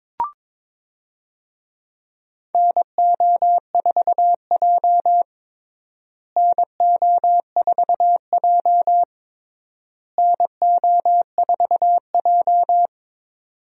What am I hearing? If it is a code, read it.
Morse code, heard as a keyed tone, about 22 wpm, 700 Hz. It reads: NO4J NO4J NO4J
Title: USA Call Signs - 4 Characters - Repeated 3x - 22wpm